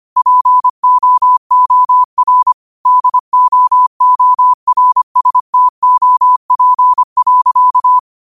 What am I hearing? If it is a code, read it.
Morse code, heard as a keyed tone, about 25 wpm, 1000 Hz. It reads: POOR DOORSTOP.
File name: qotdfiles / 25wpm